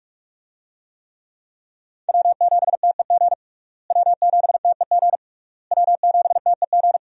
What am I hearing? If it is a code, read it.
Morse code, heard as a keyed tone, about 46 wpm, 700 Hz. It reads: W7TEG W7TEG W7TEG
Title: USA Call Signs - 5 Characters - Repeated 3x - 45wpm